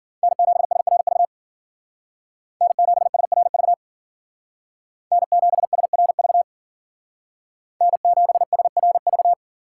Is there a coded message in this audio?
Yes: Morse code from 0.2 to 9.3 s, heard as a keyed tone, about 52 wpm, 700 Hz. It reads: N7SRV N7SRV N7SRV N7SRV